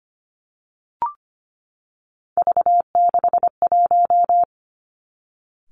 Morse code, heard as a keyed tone, about 25 wpm, 700 Hz. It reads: V61
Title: Three Challenging Letter-Number Combinations 25wpm